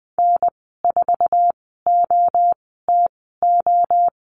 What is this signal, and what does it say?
Morse code, heard as a keyed tone, about 20 wpm, 700 Hz. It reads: N 4 O T O
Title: 048 Lesson 9g - Number 4 - Call Signs 20wpm